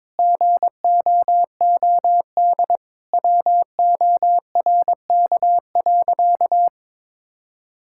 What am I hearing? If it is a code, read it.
Morse code, heard as a keyed tone, about 23 wpm, 700 Hz. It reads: GOOD WORK.
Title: Sentences from Top 1000 Words - Review Part 5 - 22wpm